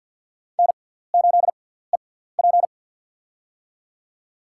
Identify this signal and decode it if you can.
Morse code, heard as a keyed tone, about 47 wpm, spaced as for 17 wpm, 700 Hz. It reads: N8EP
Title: USA Call Signs - 6x Character Spacing - 50wpm